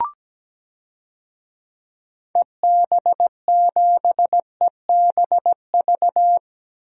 Morse code, heard as a keyed tone, about 17 wpm, 700 Hz. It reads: EB7EBV
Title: DXCC-100 Call Signs - 6 Characters - 17wpm